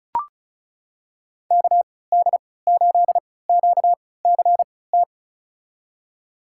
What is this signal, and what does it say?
Morse code, heard as a keyed tone, about 36 wpm, spaced as for 20 wpm, 700 Hz. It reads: KD8QCT
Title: USA Call Signs - 6 Characters - 3x Character Spacing - 35wpm